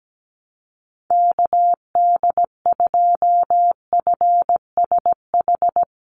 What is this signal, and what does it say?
Morse code, heard as a keyed tone, about 17 wpm, 700 Hz. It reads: KD2FSH
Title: Call Signs Repeated 3x - 17wpm